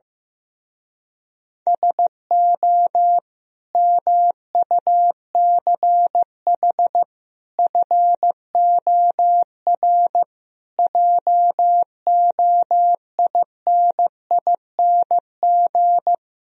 Morse code, heard as a keyed tone, about 15 wpm, 700 Hz. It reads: SO MUCH FOR JOINING
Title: Speed-Racing - Sets of 5 Words 15wpm